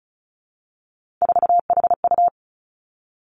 Morse code, heard as a keyed tone, about 34 wpm, 700 Hz. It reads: 4HU
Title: Three Challenging Letter-Number Combinations 35wpm